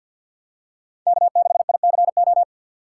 Morse code, heard as a keyed tone, about 47 wpm, 700 Hz. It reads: K6ICO
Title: USA Call Signs - 5 Characters - 2x Character Spacing - 50wpm